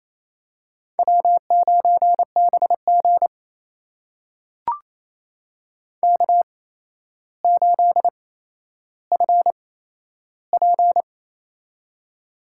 Morse code, heard as a keyed tone, about 28 wpm, spaced as for 12 wpm, 700 Hz. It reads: W9BG K8FP
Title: USA Call Signs - 4 Characters - 8x Character Spacing - 28wpm